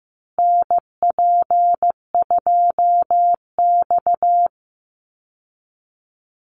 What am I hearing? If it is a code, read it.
Morse code, heard as a keyed tone, about 15 wpm, 700 Hz. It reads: NP2X
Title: Call Signs Repeated 2x - 15wpm